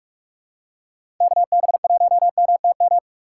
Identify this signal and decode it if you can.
Morse code, heard as a keyed tone, about 46 wpm, 700 Hz. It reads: KB1MTM
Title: USA Call Signs - 6 Characters - 4x Character Spacing - 45wpm